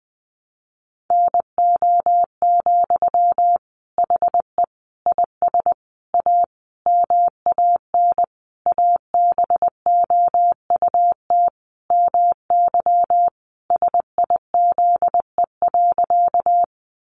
Morse code, heard as a keyed tone, about 20 wpm, 700 Hz. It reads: NO, HE IS A MAN ABOUT MY SIZE.
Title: Sentences from Top 400 Words 20wpm